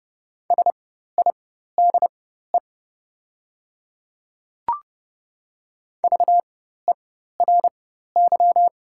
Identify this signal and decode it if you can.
Morse code, heard as a keyed tone, about 30 wpm, spaced as for 14 wpm, 690 Hz. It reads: SIDE VERY
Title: 4x Character Spacing - Top 200 Words - 30wpm